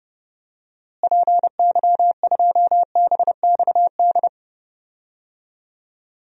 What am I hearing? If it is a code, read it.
Morse code, heard as a keyed tone, about 30 wpm, 700 Hz. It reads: PY2BXD
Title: DXCC-100 Call Signs - 6 Characters - Repeated 2x - 30wpm